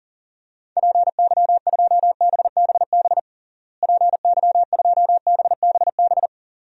Morse code, heard as a keyed tone, about 37 wpm, 700 Hz. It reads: PY2BBB PY2BBB
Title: DXCC-100 Call Signs - 6 Characters - Repeated 2x - 40wpm